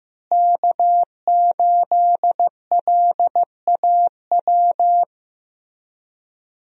Morse code, heard as a keyed tone, about 15 wpm, 700 Hz. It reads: K8LAW
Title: USA Call Signs - 5 Characters - Repeated 3x - 15wpm